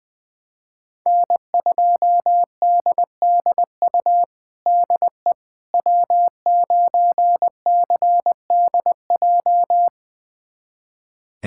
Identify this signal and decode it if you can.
Morse code, heard as a keyed tone, about 20 wpm, 700 Hz. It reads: N2DDU DE W9CDJ
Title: USA Callsign DE Callsign - 5 Characters - 20wpm